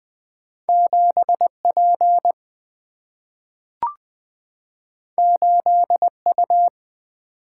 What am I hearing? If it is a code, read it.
Morse code, heard as a keyed tone, about 20 wpm, 700 Hz. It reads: 7P 8U
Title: Single-Letter Single-Number 20wpm